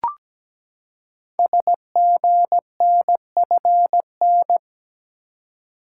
Morse code, heard as a keyed tone, about 17 wpm, 700 Hz. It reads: SGNFN
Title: Code Groups - 5 Letters - 17wpm